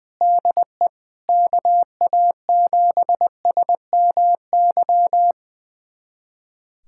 Morse code, heard as a keyed tone, about 20 wpm, 700 Hz. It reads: DE KA7SMY
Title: USA Callsign DE Callsign - 6 Characters - 20wpm